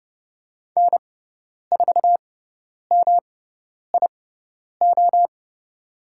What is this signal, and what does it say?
Morse code, heard as a keyed tone, about 30 wpm, spaced as for 10 wpm, 700 Hz. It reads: N4MIO
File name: Lesson 18g - Letter m - Call Signs 30wpm